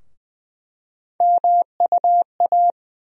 Morse code, heard as a keyed tone, about 20 wpm, 700 Hz. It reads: MUA
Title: USA Call Sign Postfixes - 3 Letters 20wpm